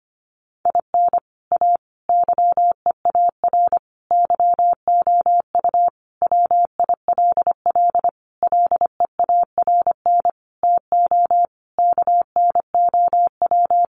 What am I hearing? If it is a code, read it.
Morse code, heard as a keyed tone, about 25 wpm, 700 Hz. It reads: IN A YEAR YOU WILL LEARN TO KNOW
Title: Sentences from Top 600 Words 25wpm